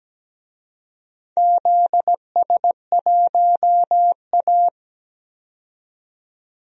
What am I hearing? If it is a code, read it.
Morse code, heard as a keyed tone, about 17 wpm, 700 Hz. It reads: ZS1A
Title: DXCC-100 Call Signs - 4 Characters - 17wpm